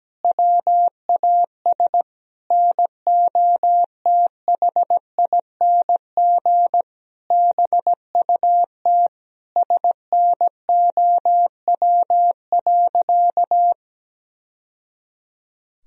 Morse code, heard as a keyed tone, about 17 wpm, 700 Hz. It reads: WAS NOTHING BUT SNOW.
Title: Sentences from Top 1000 Words - Review Part 3 - 17wpm